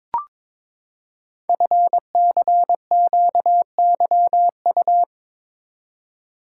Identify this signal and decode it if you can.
Morse code, heard as a keyed tone, about 22 wpm, 700 Hz. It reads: FCQYU
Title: Code Groups - 5 Letters - 22wpm